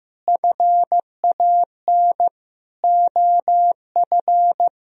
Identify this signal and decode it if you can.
Morse code, heard as a keyed tone, about 15 wpm, 700 Hz. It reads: FAN OF
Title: Sentences from Top 1000 Words - Review Part 4 - 15wpm